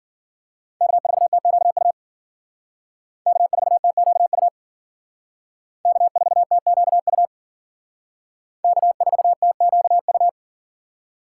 Morse code, heard as a keyed tone, about 50 wpm, 700 Hz. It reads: K4TQU K4TQU K4TQU K4TQU
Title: Speed Racing - USA 1x3 Call Signs 40wpm